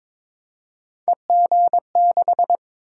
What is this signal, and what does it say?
Morse code, heard as a keyed tone, about 22 wpm, 700 Hz. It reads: EG6
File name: practice / 22wpm / Three Letter-Number Combinations - 22wpm